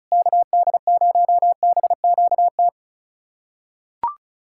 Morse code, heard as a keyed tone, about 34 wpm, 700 Hz. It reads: KD0BQT
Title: USA Callsign DE Callsign - 6 Characters - 35wpm